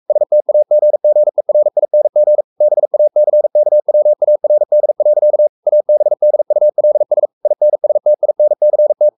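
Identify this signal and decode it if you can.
Morse code, heard as a keyed tone, about 44 wpm, 600 Hz. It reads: STAGGERING BACKWARD. ABDULS INSTINCT
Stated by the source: Morse code at 43 wpm sending Ken Follett's novel Never in English, from CW med Gnister.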